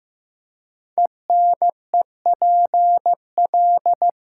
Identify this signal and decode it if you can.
Morse code, heard as a keyed tone, about 15 wpm, 700 Hz. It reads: ENEPL
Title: Code Groups - 5 Letters - 15wpm